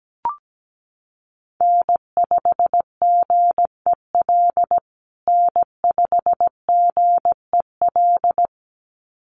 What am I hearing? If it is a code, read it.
Morse code, heard as a keyed tone, about 17 wpm, 700 Hz. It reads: N5GEL N5GEL
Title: USA Call Signs Repeated 2x 17wpm